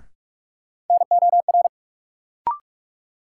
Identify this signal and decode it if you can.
Morse code, heard as a keyed tone, about 46 wpm, 700 Hz. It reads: NOR
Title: Sentences from Top 1500 Words - New Word Drill Alt - 45wpm